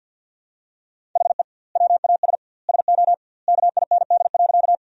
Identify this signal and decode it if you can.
Morse code, heard as a keyed tone, about 50 wpm, 700 Hz. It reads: HE WAS SO KIND.
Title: 2x Word-Spacing - Sentences from Top 200 Words - 50wpm